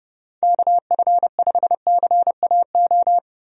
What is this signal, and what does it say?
Morse code, heard as a keyed tone, about 30 wpm, 700 Hz. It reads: KF5CAO
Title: USA Call Signs - 6 Characters - Repeated 3x - 30wpm